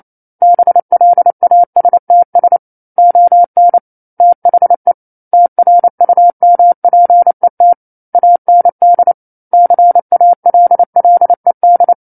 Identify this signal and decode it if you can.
Morse code, heard as a keyed tone, about 28 wpm, 700 Hz. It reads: BLASTS ON THE TRUMPET AND CALLED
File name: 09-Chapter-09 28 WPM